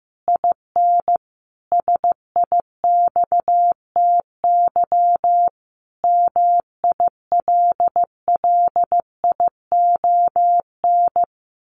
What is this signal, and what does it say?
Morse code, heard as a keyed tone, about 15 wpm, 700 Hz. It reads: IN SIXTY MILLION
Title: A Sound Of Thunder 15wpm